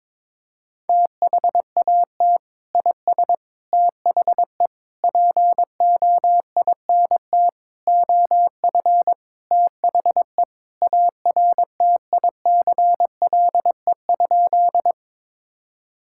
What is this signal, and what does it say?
Morse code, heard as a keyed tone, about 21 wpm, 700 Hz. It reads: THAT IS THE POINT OF THE ARTICLE?
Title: Sentences from Top 1000 Words - Review Part 5 - 22wpm